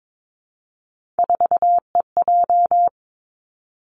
Morse code, heard as a keyed tone, about 22 wpm, 700 Hz. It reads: 4EJ
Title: Three Challenging Letter-Number Combinations 22wpm